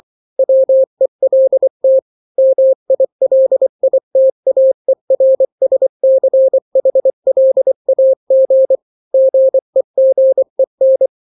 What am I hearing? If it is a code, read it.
Morse code, heard as a keyed tone, about 24 wpm, 530 Hz. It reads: WELT MILITAERSCHLAG GEGEN
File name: daily / 23wpm